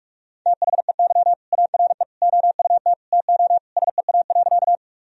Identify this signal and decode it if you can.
Morse code, heard as a keyed tone, about 46 wpm, 700 Hz. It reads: THEY ARE OUT TO SEA.
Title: Sentences from Top 1000 Words - Review Part 1 - 45wpm